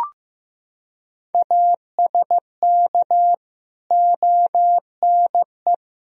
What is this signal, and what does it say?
Morse code, heard as a keyed tone, about 15 wpm, 700 Hz. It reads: ASK ONE
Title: Sentences from Top 1000 Words - Review Part 1 - 15wpm